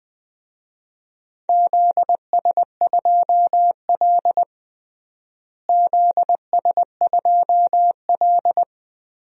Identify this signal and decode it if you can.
Morse code, heard as a keyed tone, about 20 wpm, 700 Hz. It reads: ZS2L ZS2L
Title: DXCC-100 Call Signs - 4 Characters - Repeated 3x - 20wpm